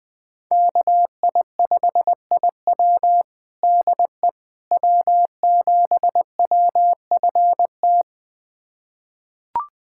Morse code, heard as a keyed tone, about 20 wpm, 700 Hz. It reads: KI5IW DE W7WFT
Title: USA Callsign DE Callsign 20wpm